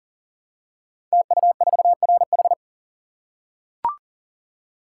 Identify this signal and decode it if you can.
Morse code, heard as a keyed tone, about 39 wpm, 700 Hz. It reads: TU4RH
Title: Code Groups - 5 Letter:Numbers - 40wpm